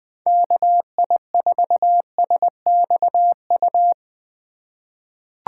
Morse code, heard as a keyed tone, about 20 wpm, 700 Hz. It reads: KI4SXU
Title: USA Call Signs - 6 Characters - 3x Character Spacing - 20wpm